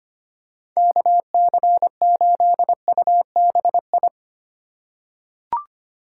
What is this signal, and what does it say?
Morse code, heard as a keyed tone, about 25 wpm, 700 Hz. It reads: KC8UBI
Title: USA Call Signs - 6 Characters - 4x Character Spacing - 25wpm